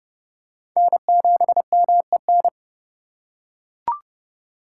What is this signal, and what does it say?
Morse code, heard as a keyed tone, about 30 wpm, 700 Hz. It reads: N7MEN
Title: USA Call Signs - 5 Characters - 2x Character Spacing - 30wpm